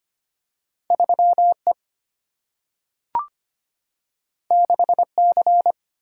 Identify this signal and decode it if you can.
Morse code, heard as a keyed tone, about 25 wpm, 690 Hz. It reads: 3E 6C
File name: Two Letter-Number Combinations - 25wpm